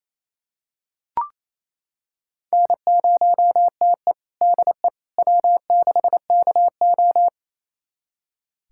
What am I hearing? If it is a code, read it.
Morse code, heard as a keyed tone, about 28 wpm, 700 Hz. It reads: N0TE DE W6KO
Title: USA Callsign DE Callsign - 4 Characters - 28wpm